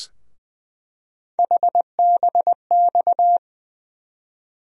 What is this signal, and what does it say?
Morse code, heard as a keyed tone, about 20 wpm, 700 Hz. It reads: HBX
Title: USA Call Sign Postfixes - 3 Letters 20wpm